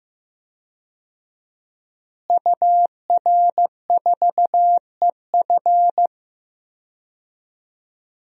Morse code, heard as a keyed tone, about 15 wpm, 700 Hz. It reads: UR4EF
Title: DXCC-100 Call Signs - 5 Characters - Repeated 2x - 15wpm